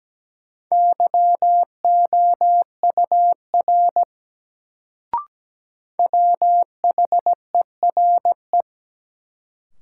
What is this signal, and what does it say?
Morse code, heard as a keyed tone, about 17 wpm, 700 Hz. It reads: YOUR WHERE
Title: Sets of 2 Words - New Word Drill - 17wpm